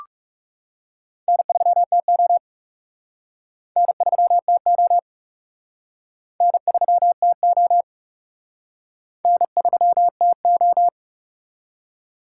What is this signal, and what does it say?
Morse code, heard as a keyed tone, about 35 wpm, 700 Hz. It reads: N3TO N3TO N3TO N3TO
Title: Speed Racing - USA Call Signs 30wpm